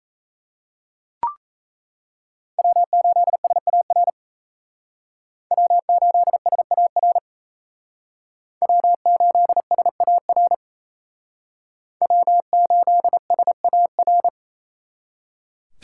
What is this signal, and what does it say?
Morse code, heard as a keyed tone, about 37 wpm, 700 Hz. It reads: W8SAR W8SAR W8SAR W8SAR